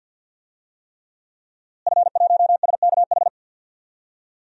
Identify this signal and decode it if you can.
Morse code, heard as a keyed tone, about 46 wpm, 700 Hz. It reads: F1SKH
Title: DXCC-100 Call Signs - 5 Characters - Repeated 3x - 50wpm